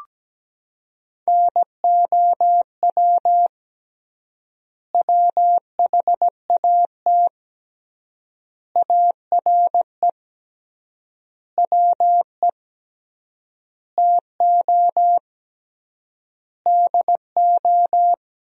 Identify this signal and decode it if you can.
Morse code, heard as a keyed tone, about 17 wpm, 700 Hz. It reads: NOW WHAT ARE WE TO DO
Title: 3x Word-Spacing - Sentences from Top 100 Words - 17wpm